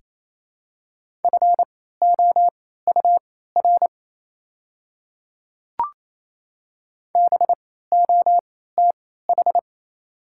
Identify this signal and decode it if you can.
Morse code, heard as a keyed tone, about 28 wpm, spaced as for 16 wpm, 700 Hz. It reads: FOUR BOTH
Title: 3x Character Spacing - Top 500 Words - 28wpm